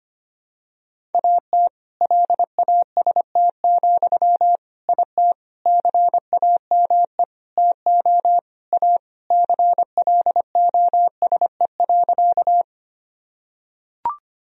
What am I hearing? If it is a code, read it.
Morse code, heard as a keyed tone, about 25 wpm, 700 Hz. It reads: AT LAST, IT CAME TO A CLOSE.
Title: Sentences from Top 1000 Words - Review Part 1 - 25wpm